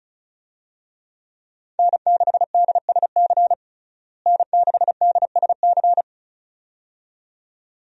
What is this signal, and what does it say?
Morse code, heard as a keyed tone, about 36 wpm, 700 Hz. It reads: N6DSC N6DSC
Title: USA Call Signs - 5 Characters - Repeated 2x - 35wpm